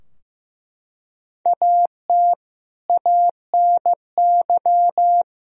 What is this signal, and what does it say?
Morse code, heard as a keyed tone, about 15 wpm, 700 Hz. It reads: AT ANY
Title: Sentences from Top 1000 Words - Review Part 4 - 15wpm